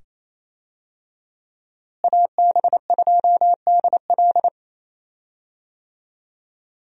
Morse code, heard as a keyed tone, about 28 wpm, 700 Hz. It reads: AB2DL
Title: USA Call Signs - 5 Characters - Repeated 2x - 28wpm